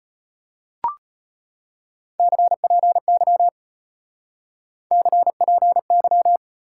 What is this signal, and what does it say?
Morse code, heard as a keyed tone, about 33 wpm, 700 Hz. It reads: CPY CPY